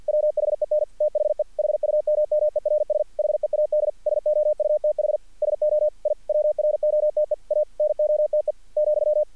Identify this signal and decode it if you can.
Morse code, heard as a keyed tone, about 46 wpm, 600 Hz. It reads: WHEN THE HUMMERS HEAD SOUTH SO I WROTE A NOTE ,
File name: archived / CRSnet